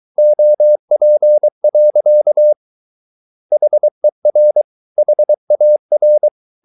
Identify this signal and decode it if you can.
Morse code, heard as a keyed tone, about 23 wpm, 600 Hz. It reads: OP. HER HAR
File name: Ferie QRS 23wpm